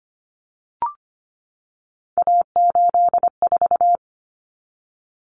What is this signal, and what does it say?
Morse code, heard as a keyed tone, about 25 wpm, 700 Hz. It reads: A84
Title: Common Cut Numbers - 3-Digit 25wpm